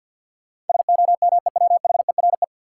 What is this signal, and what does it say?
Morse code, heard as a keyed tone, about 46 wpm, 700 Hz. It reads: SOMEWHERE
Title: Sentences from Top 1400 Words - New Word Drill - 50wpm